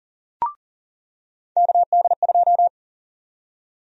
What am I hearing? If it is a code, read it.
Morse code, heard as a keyed tone, about 39 wpm, 700 Hz. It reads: KD2
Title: USA Call Sign Prefixes - 40wpm